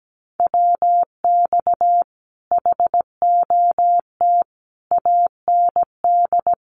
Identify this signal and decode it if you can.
Morse code, heard as a keyed tone, about 17 wpm, 700 Hz. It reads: WX HOT AND
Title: Common QSO Phrases Encore 17wpm